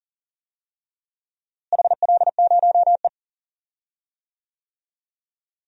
Morse code, heard as a keyed tone, about 37 wpm, 700 Hz. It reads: HL0E
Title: DXCC-100 Call Signs - 4 Characters - Repeated 2x - 40wpm